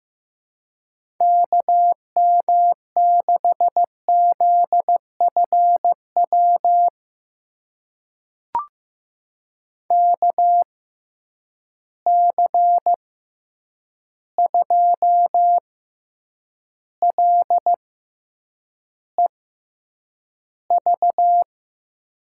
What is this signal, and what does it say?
Morse code, heard as a keyed tone, about 15 wpm, spaced as for 8 wpm, 700 Hz. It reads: KM6ZFW KC2LEV
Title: USA Call Signs - 6 Characters - 6x Character Spacing - 15wpm